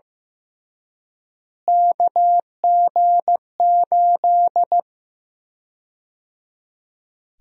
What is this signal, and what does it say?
Morse code, heard as a keyed tone, about 15 wpm, 700 Hz. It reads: KG8
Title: USA Call Sign Prefixes - Repeated 2x - 15wpm